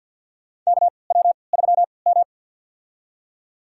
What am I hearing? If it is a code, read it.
Morse code, heard as a keyed tone, about 50 wpm, spaced as for 28 wpm, 700 Hz. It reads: KW3M
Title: USA Call Signs - 3x Character Spacing - 50wpm